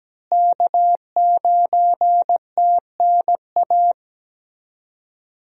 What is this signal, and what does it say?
Morse code, heard as a keyed tone, about 17 wpm, 700 Hz. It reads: K9TNA
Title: USA Call Signs - 5 Characters - Repeated 3x - 17wpm